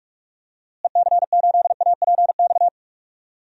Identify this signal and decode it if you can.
Morse code, heard as a keyed tone, about 43 wpm, 700 Hz. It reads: EC8APX
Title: DXCC-100 Call Signs - 6 Characters - Repeated 3x - 45wpm